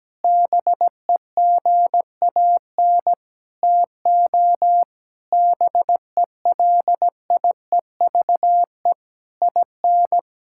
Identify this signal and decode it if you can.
Morse code, heard as a keyed tone, about 17 wpm, 700 Hz. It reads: BEGAN TO BELIEVE IN
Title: Sentences from Top 1600 Words 17wpm